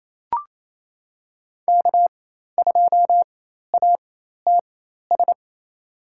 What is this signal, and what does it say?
Morse code, heard as a keyed tone, about 28 wpm, spaced as for 13 wpm, 700 Hz. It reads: K2ATS
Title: USA Call Signs - 5 Characters - 4x Character Spacing - 28wpm